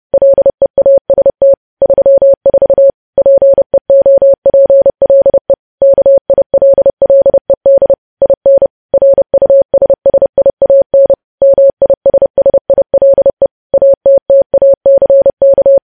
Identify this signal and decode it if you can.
Morse code, heard as a keyed tone, about 30 wpm, 550 Hz. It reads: LEAST 34 PEOPLE KILLED IN RUSSIAN MISSILE ATTACK
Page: news headlines in morse code at 30 wpm